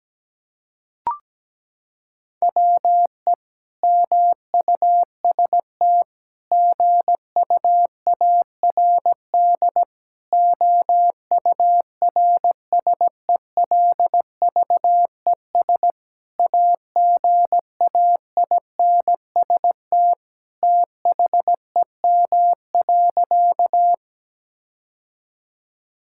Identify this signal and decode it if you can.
Morse code, heard as a keyed tone, about 17 wpm, 700 Hz. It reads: WE MUST GUARD OURSELVES AGAINST THEM.
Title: Sentences from Top 1600 Words 17wpm